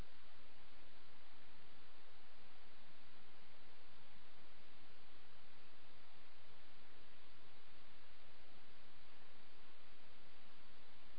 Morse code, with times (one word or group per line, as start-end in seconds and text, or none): none